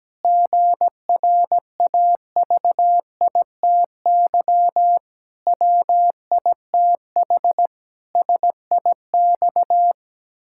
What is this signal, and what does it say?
Morse code, heard as a keyed tone, about 17 wpm, 700 Hz. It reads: GRAVITY WITH SIX
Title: Pangram Challenge v2 - 17wpm